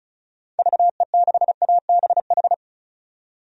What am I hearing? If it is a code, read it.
Morse code, heard as a keyed tone, about 36 wpm, 700 Hz. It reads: VE6ABH
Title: DXCC-100 Call Signs - 6 Characters - 35wpm